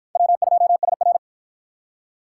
Morse code, heard as a keyed tone, about 47 wpm, 700 Hz. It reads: W2SR